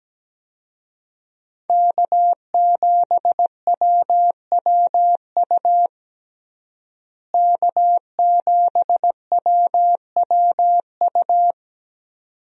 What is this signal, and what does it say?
Morse code, heard as a keyed tone, about 17 wpm, 700 Hz. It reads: K7WWU K7WWU
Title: USA Call Signs - 5 Characters - Repeated 2x - 17wpm